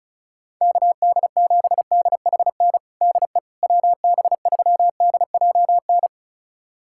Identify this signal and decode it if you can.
Morse code, heard as a keyed tone, about 36 wpm, 700 Hz. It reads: KD7DHN DE WB3DJN
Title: USA Callsign DE Callsign 35wpm